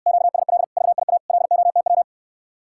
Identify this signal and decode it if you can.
Morse code, heard as a keyed tone, about 69 wpm, 700 Hz. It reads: QSL VIA LOTW